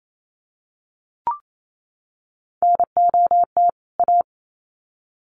Morse code, heard as a keyed tone, about 28 wpm, 700 Hz. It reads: NOT A
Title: Sets of 2 Words 28wpm